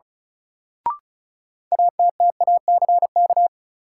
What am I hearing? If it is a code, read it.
Morse code, heard as a keyed tone, about 36 wpm, 700 Hz. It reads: ATTACK
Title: Sentences from Top 1500 Words - New Word Drill Alt - 35wpm